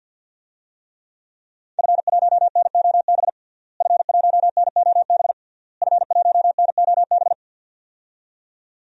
Morse code, heard as a keyed tone, about 49 wpm, 700 Hz. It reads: F1NOB F1NOB F1NOB
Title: DXCC-100 Call Signs - 5 Characters - Repeated 3x - 50wpm